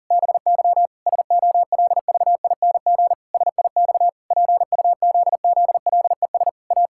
Morse code, heard as a keyed tone, about 40 wpm, 700 Hz. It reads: BY SOLVING SIX PUZZLES A
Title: Pangram Challenge v2 - 40wpm